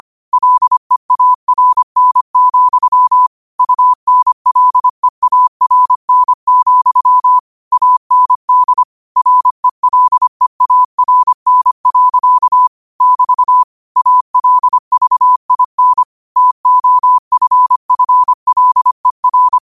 Morse code, heard as a keyed tone, about 25 wpm, 1000 Hz. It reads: LEARN, UNLEARN, AND RELEARN. <BT> ALVIN TOFFLER